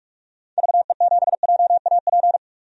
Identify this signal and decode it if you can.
Morse code, heard as a keyed tone, about 42 wpm, 690 Hz. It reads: VE7JAP
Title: DXCC-100 Call Signs - 6 Characters - 45wpm